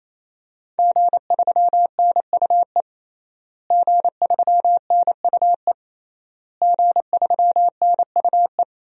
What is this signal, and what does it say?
Morse code, heard as a keyed tone, about 28 wpm, 700 Hz. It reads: G3NUE G3NUE G3NUE
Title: DXCC-100 Call Signs - 5 Characters - Repeated 3x - 28wpm